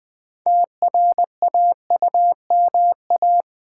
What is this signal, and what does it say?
Morse code, heard as a keyed tone, about 20 wpm, 700 Hz. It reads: TRAUMA
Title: 6 Letter Words - First 3 Letters Given - 20wpm